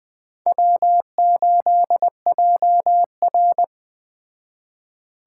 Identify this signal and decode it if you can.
Morse code, heard as a keyed tone, about 20 wpm, 700 Hz. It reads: W8JR